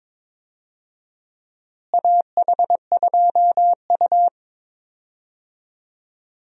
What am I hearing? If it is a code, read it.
Morse code, heard as a keyed tone, about 22 wpm, 700 Hz. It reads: AH2U
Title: USA Call Signs - 4 Characters - Repeated 3x - 22wpm